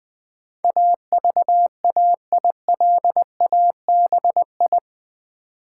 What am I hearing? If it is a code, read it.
Morse code, heard as a keyed tone, about 20 wpm, 700 Hz. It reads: AVAILABI
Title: Build A Word - 4 Letters At A Time - 12-Letter Words - 20wpm